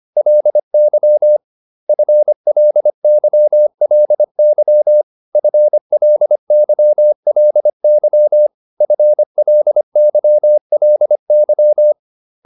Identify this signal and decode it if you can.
Morse code, heard as a keyed tone, about 25 wpm, 600 Hz. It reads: LY FLYLY FLYLY FLYLY